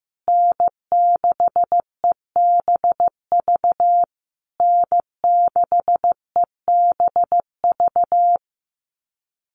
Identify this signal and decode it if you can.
Morse code, heard as a keyed tone, about 15 wpm, 700 Hz. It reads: N6EBV N6EBV
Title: USA Call Signs Repeated 3x 15wpm